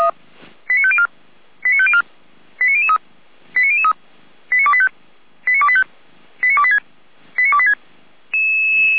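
Five-tone selective call (ZVEI), heard as quick runs of five tones, tones of about 70 ms. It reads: 89504 89504 89003 89003 89285 89285 89286 89286
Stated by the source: ZVEI five-tone selective call received on SDR, from feuerwehr-orlamuende fire brigade.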